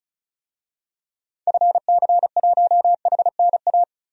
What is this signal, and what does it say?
Morse code, heard as a keyed tone, about 34 wpm, 700 Hz. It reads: FC1HNA
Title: DXCC-100 Call Signs - 6 Characters - 35wpm